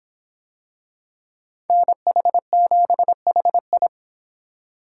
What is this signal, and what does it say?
Morse code, heard as a keyed tone, about 26 wpm, 700 Hz. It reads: NH7HI